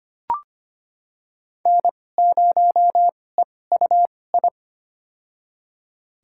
Morse code, heard as a keyed tone, about 25 wpm, spaced as for 18 wpm, 700 Hz. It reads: N0EUI
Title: USA Call Signs - 5 Characters - 2x Character Spacing - 25wpm